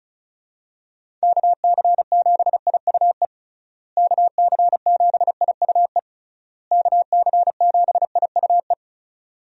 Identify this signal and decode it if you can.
Morse code, heard as a keyed tone, about 36 wpm, 700 Hz. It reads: KC7IUE KC7IUE KC7IUE
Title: USA Call Signs - 6 Characters - Repeated 3x - 35wpm